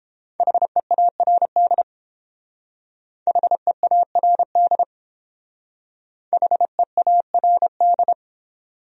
Morse code, heard as a keyed tone, about 30 wpm, 700 Hz. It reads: HEARD HEARD HEARD